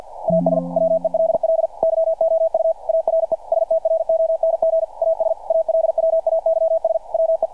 Morse code, heard as a keyed tone, about 46 wpm, 590 Hz. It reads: AS WELL YOU ARE STRONG AS ALWAYS WE
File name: archived / CRSnet